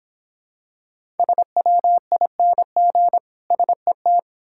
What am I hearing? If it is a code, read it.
Morse code, heard as a keyed tone, about 25 wpm, 700 Hz. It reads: SWING SET